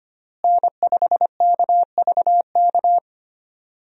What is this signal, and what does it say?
Morse code, heard as a keyed tone, about 25 wpm, 700 Hz. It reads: N5KVK
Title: USA Call Signs - 5 Characters - Repeated 3x - 25wpm